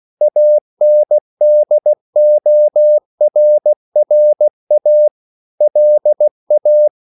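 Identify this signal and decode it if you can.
Morse code, heard as a keyed tone, about 16 wpm, 600 Hz. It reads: ANDORRA LA
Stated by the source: CW practice of national capitals at 16 words per minute, from ON6ZQ.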